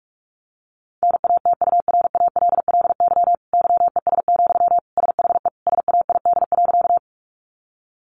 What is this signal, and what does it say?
Morse code, heard as a keyed tone, about 44 wpm, 700 Hz. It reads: NATURALLY YES, SHE SAID.
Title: Sentences from Top 1300 Words 45wpm